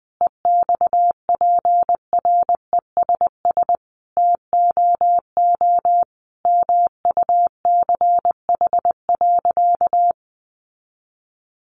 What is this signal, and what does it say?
Morse code, heard as a keyed tone, about 20 wpm, 700 Hz. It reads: EXPRESS TOO MUCH.
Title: Sentences from Top 1100 Words 20wpm